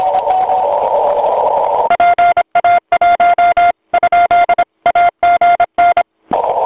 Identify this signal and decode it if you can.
Morse code, heard as a keyed tone, about 25 wpm, 1400 Hz, with noise. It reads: PA1? AGN
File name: voice / 80m j3 pa5et 002 2001 08 08